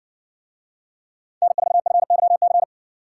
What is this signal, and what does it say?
Morse code, heard as a keyed tone, about 60 wpm, 700 Hz. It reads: N4FYC